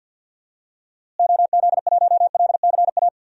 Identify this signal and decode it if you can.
Morse code, heard as a keyed tone, about 53 wpm, 700 Hz. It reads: OZ1LCA
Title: DXCC-100 Call Signs - 6 Characters - Repeated 2x - 50wpm